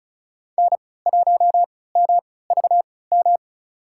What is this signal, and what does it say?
Morse code, heard as a keyed tone, about 36 wpm, spaced as for 20 wpm, 700 Hz. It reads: N1MVM
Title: USA Call Signs - 5 Characters - 3x Character Spacing - 35wpm